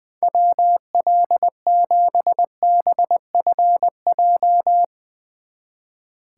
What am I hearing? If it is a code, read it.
Morse code, heard as a keyed tone, about 20 wpm, 700 Hz. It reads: WL7BFJ
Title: Speed Racing - USA 2x3 Call Signs 20wpm